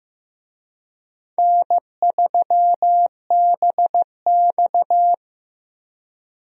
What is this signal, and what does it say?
Morse code, heard as a keyed tone, about 15 wpm, 700 Hz. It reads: N3BX
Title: USA Call Signs - 4 Characters - Repeated 2x - 15wpm